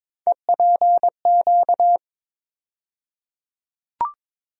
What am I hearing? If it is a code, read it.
Morse code, heard as a keyed tone, about 22 wpm, 700 Hz. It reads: EPQ
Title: USA Call Sign Postfixes - 3 Letters 22wpm